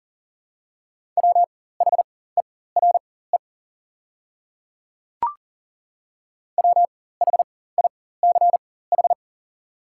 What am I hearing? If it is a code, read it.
Morse code, heard as a keyed tone, about 40 wpm, spaced as for 19 wpm, 700 Hz. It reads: WHERE WHICH
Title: 4x Character Spacing - Top 200 Words - 40wpm